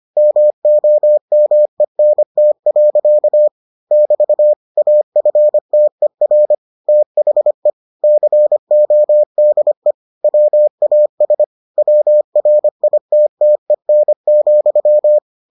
Morse code, heard as a keyed tone, about 25 wpm, 600 Hz. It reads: MOMENT. <BT> AFTER THE CODE WAS WRITTEN,